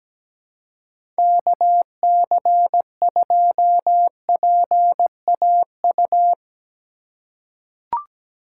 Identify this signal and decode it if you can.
Morse code, heard as a keyed tone, about 17 wpm, 700 Hz. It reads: KC2PAU
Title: USA Call Signs - 6 Characters - 4x Character Spacing - 17wpm